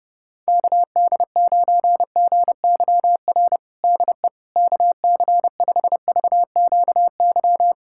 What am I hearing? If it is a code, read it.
Morse code, heard as a keyed tone, about 32 wpm, 700 Hz. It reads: KD9GYR DE KC5VQY
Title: USA Callsign DE Callsign - 6 Characters - 30wpm